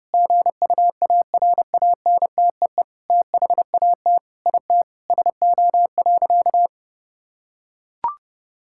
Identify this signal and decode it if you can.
Morse code, heard as a keyed tone, about 30 wpm, 700 Hz. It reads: GUARANTEE THAT IT SO.
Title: Sentences from Top 1600 Words 30wpm